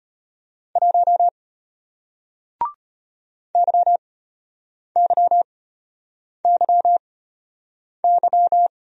Morse code, heard as a keyed tone, about 34 wpm, spaced as for 8 wpm, 700 Hz. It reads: 1 YYYY